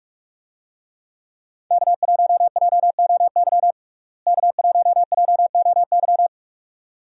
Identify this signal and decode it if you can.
Morse code, heard as a keyed tone, about 47 wpm, 700 Hz. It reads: K1JOY K1JOY
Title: USA Call Signs - 5 Characters - Repeated 2x - 45wpm